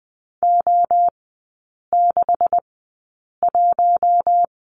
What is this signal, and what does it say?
Morse code, heard as a keyed tone, about 20 wpm, spaced as for 8 wpm, 700 Hz. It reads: O61
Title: 138 Lesson 23e - Letter f - Three Characters 20wpm